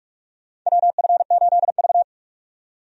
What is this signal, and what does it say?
Morse code, heard as a keyed tone, about 46 wpm, 700 Hz. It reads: WF8V
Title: USA Call Signs - 4 Characters - 45wpm